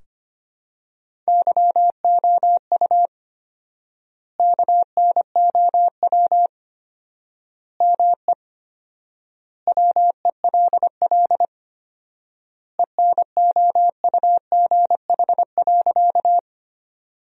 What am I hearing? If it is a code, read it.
Morse code, heard as a keyed tone, about 25 wpm, 700 Hz. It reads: YOU KNOW ME WELL ENOUGH.
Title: 4x Word-Spacing - Sentences from Top 400 Words - 25wpm